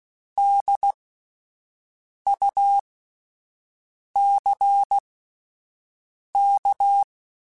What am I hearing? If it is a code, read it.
Morse code, heard as a keyed tone, about 16 wpm, spaced as for 5 wpm, 800 Hz. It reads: DUCK